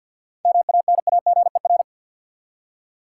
Morse code, heard as a keyed tone, about 50 wpm, 700 Hz. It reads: MANAGER
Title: Seven Letter Words - 50wpm